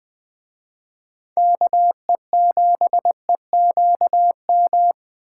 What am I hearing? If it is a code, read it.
Morse code, heard as a keyed tone, about 20 wpm, 700 Hz. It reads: KE7EQM
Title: Speed Racing - USA 2x3 Call Signs 17wpm